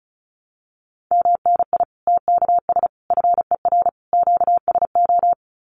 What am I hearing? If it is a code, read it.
Morse code, heard as a keyed tone, about 34 wpm, 700 Hz. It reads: MNI TKS FER QSO
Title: Common QSO Phrases 35wpm